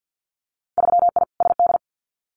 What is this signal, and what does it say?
Morse code, heard as a keyed tone, about 50 wpm, 700 Hz. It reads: 3I SD
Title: Winter Field Day Exchanges Encore 50wpm